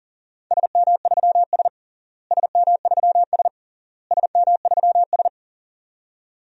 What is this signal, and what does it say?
Morse code, heard as a keyed tone, about 37 wpm, 700 Hz. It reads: SM3S SM3S SM3S
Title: DXCC-100 Call Signs - 4 Characters - Repeated 3x - 40wpm